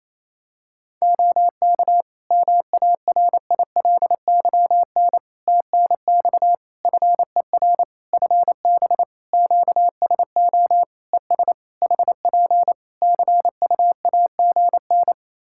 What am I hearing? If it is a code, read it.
Morse code, heard as a keyed tone, about 27 wpm, 700 Hz. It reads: OK MARILYN TNX FER FB QSO ES HP CUAGN
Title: Practice QSO - Exchange 4 - Ending - 28wpm